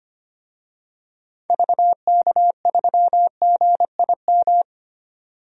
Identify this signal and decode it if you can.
Morse code, heard as a keyed tone, about 25 wpm, 700 Hz. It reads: VK3GIM
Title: DXCC-100 Call Signs - 6 Characters - 25wpm